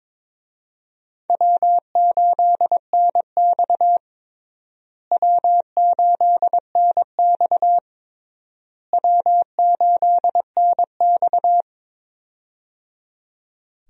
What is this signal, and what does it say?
Morse code, heard as a keyed tone, about 22 wpm, 700 Hz. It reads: W8NX W8NX W8NX
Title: USA Call Signs - 4 Characters - Repeated 3x - 22wpm